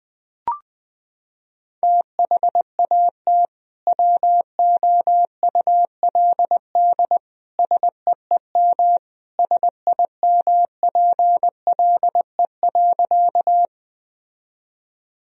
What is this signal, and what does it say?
Morse code, heard as a keyed tone, about 21 wpm, 700 Hz. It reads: THAT WOULD SEEM SIMPLE.
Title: Sentences from Top 1000 Words - Review Part 1 - 20wpm